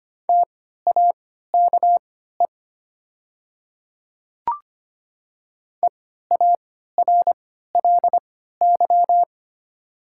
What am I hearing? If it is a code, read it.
Morse code, heard as a keyed tone, about 25 wpm, spaced as for 14 wpm, 700 Hz. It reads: TAKE EARLY
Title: 3x Character Spacing - Top 500 Words - 25wpm